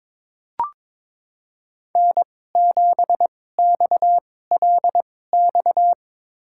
Morse code, heard as a keyed tone, about 22 wpm, spaced as for 16 wpm, 700 Hz. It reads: N7XLX
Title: USA Call Signs - 5 Characters - 2x Character Spacing - 22wpm